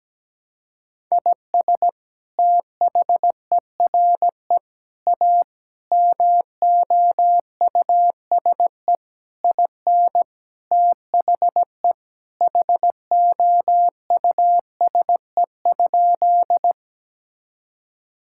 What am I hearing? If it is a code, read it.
Morse code, heard as a keyed tone, about 17 wpm, 700 Hz. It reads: IS THERE A MOUSE IN THE HOUSE?
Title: Sentences from Top 1000 Words - Review Part 5 - 17wpm